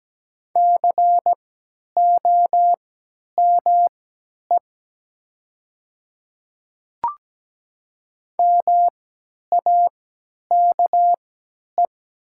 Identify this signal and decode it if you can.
Morse code, heard as a keyed tone, about 17 wpm, spaced as for 10 wpm, 700 Hz. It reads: COME MAKE
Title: 3x Character Spacing - Top 100 Words - 17wpm